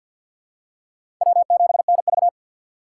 Morse code, heard as a keyed tone, about 53 wpm, 700 Hz. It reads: W7NV
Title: USA Call Signs - 6x Character Spacing - 50wpm